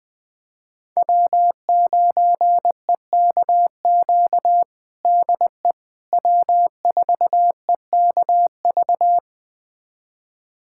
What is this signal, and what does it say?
Morse code, heard as a keyed tone, about 20 wpm, 700 Hz. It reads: W9EKQ DE W4EKV
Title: USA Callsign DE Callsign - 5 Characters - 20wpm